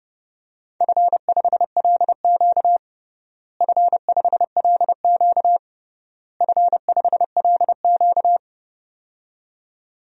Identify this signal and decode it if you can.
Morse code, heard as a keyed tone, about 32 wpm, 700 Hz. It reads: F5LQ F5LQ F5LQ
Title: DXCC-100 Call Signs - 4 Characters - Repeated 3x - 30wpm